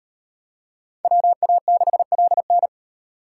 Morse code, heard as a keyed tone, about 38 wpm, 700 Hz. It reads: WA6LN